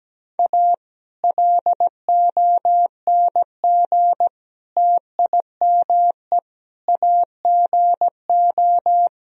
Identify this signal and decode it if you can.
Morse code, heard as a keyed tone, about 17 wpm, 700 Hz. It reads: A LONG TIME AGO
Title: Sentences from Top 1000 Words - Review Part 3 - 17wpm